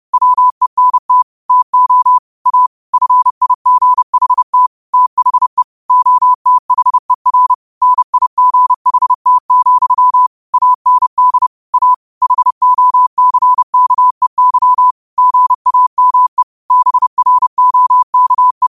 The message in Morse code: WENT TO A FIGHT THE OTHER NIGHT, AND A HOCKEY GAME BROKE

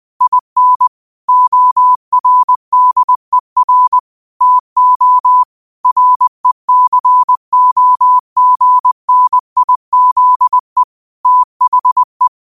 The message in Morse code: IN ORDER TO RECOGNIZE THE

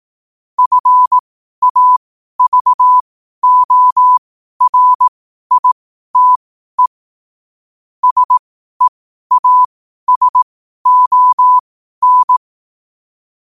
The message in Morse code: FAVORITE SEASON

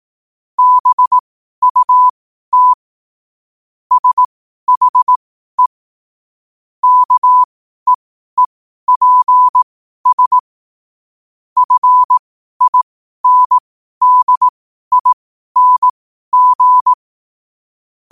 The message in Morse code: BUT SHE KEEPS FINDING

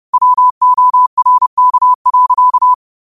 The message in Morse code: WORK.

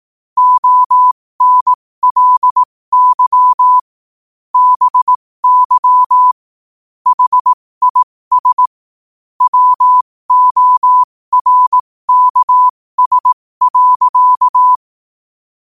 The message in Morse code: ONLY BY HIS WORKS.

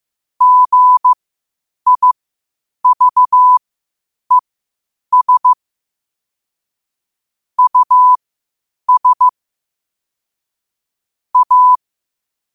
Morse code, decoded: GIVES US A

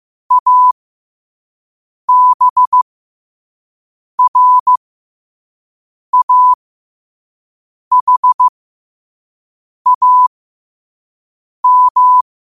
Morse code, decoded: ABRAHAM

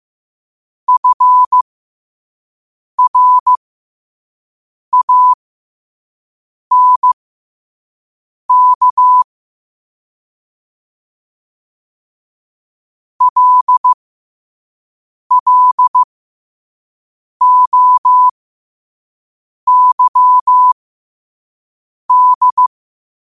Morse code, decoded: FRANK LLOYD

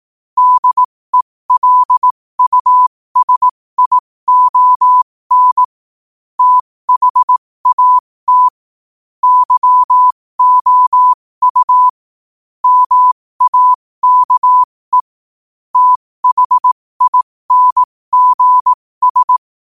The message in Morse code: DELUSION THAT YOU MAKE THINGS

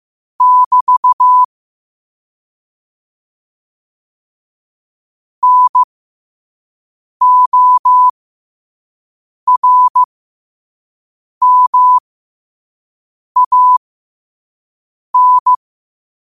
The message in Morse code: <BT> NORMAN

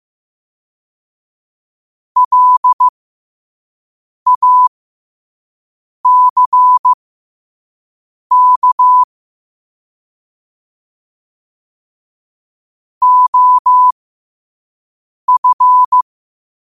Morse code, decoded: LACK OF

